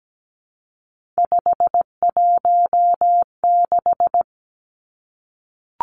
Morse code, heard as a keyed tone, about 17 wpm, 700 Hz. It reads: 516
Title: Numbers - 3-Digit 17wpm